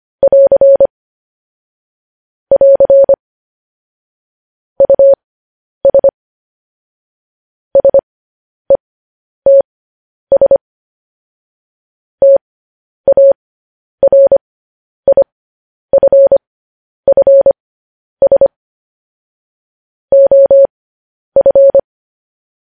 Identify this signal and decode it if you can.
Morse code, heard as a keyed tone, about 25 wpm, spaced as for 10 wpm, 550 Hz. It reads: <AR> <AR> US SETS TARIFFS OF
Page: News Headlines in Morse Code at 10 WPM